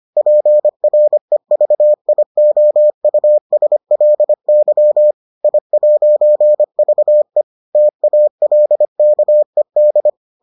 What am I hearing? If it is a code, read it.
Morse code, heard as a keyed tone, about 25 wpm, 600 Hz. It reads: PREVIOUSLY I'VE TALKED